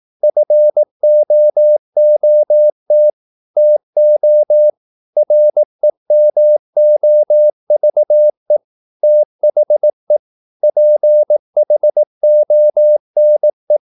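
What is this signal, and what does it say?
Morse code, heard as a keyed tone, about 18 wpm, 600 Hz. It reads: FOOT TO REMOVE THE PHONE
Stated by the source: Morse practice at 18 wpm, English text, from CW med Gnister.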